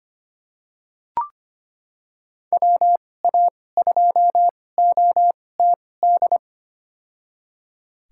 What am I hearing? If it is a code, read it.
Morse code, heard as a keyed tone, about 25 wpm, spaced as for 18 wpm, 700 Hz. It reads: WA2OTD
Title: USA Call Signs - 6 Characters - 2x Character Spacing - 25wpm